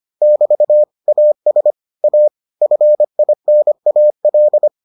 Morse code, heard as a keyed tone, about 25 wpm, 600 Hz. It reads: <BT> AS A FINAL